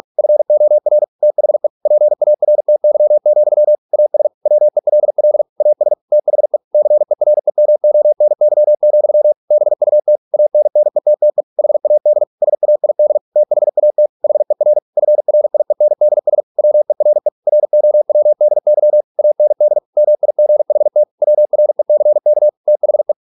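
Morse code, heard as a keyed tone, about 46 wpm, 600 Hz. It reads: FOR THE PARTY, AS WELL AS THE CEREMONY, BUT ANNETTE HAD SAID THAT HER FRIENDS WERE ROWDY AND MIGHT WRECK THE
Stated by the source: Morse practice at 45 wpm, the novel Never by Ken Follett in English, from CW med Gnister.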